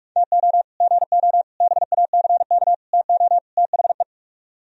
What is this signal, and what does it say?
Morse code, heard as a keyed tone, about 42 wpm, 700 Hz. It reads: TO GO BACK TO THE